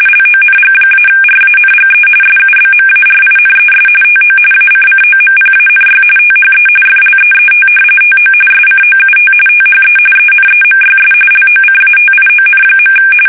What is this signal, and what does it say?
RTTY, about 45.45 baud. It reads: LONG-RANGE SUBMARINES TO ACCOMPANY THE FLEET TO SCOUT THE ENEMYS LOCATION.
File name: fsk850